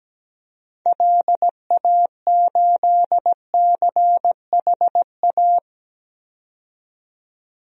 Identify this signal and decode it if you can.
Morse code, heard as a keyed tone, about 17 wpm, 700 Hz. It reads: LA8CHA
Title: DXCC-100 Call Signs - 6 Characters - 17wpm